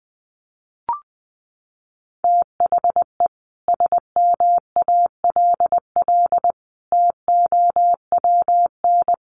Morse code, heard as a keyed tone, about 19 wpm, 700 Hz. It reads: THE SMALL TOWN
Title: Sentences from Top 1300 Words 20wpm